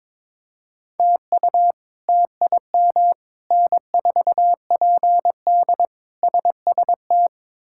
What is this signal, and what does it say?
Morse code, heard as a keyed tone, about 22 wpm, 700 Hz. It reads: TU TIM N4PD SST
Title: K1USN SST QSO Practice - 22wpm